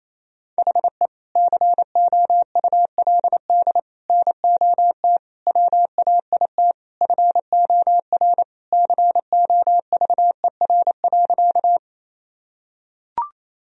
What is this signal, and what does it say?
Morse code, heard as a keyed tone, about 28 wpm, 700 Hz. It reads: HE COULD NOT WAIT FOR COVER.
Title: Sentences from Top 1000 Words - Review Part 1 - 28wpm